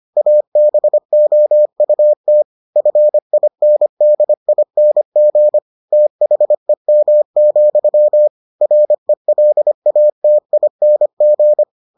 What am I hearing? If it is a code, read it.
Morse code, heard as a keyed tone, about 25 wpm, 600 Hz. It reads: ABOUT FINDING THEM, RELATING